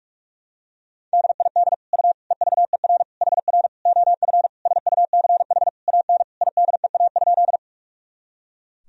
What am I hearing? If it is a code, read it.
Morse code, heard as a keyed tone, about 41 wpm, 700 Hz. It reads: DID U EVER HR OF SUCH AN IDEA?
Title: Sentences from Top 400 Words and Abbreviations 45wpm